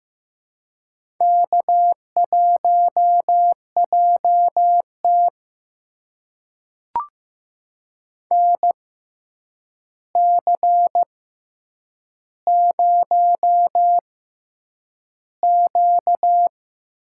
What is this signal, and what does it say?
Morse code, heard as a keyed tone, about 15 wpm, spaced as for 8 wpm, 700 Hz. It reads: K1JT NC0Q
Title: USA Call Signs - 4 Characters - 6x Character Spacing - 15wpm